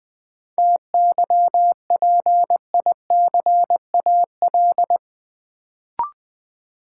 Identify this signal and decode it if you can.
Morse code, heard as a keyed tone, about 20 wpm, 700 Hz. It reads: TYPICAL
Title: Sentences from Top 1100 Words - New Word Drill Alt - 20wpm